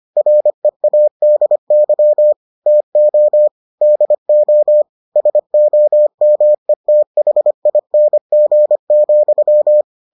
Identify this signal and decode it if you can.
Morse code, heard as a keyed tone, about 25 wpm, 600 Hz. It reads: READY TO DO SOMETHING,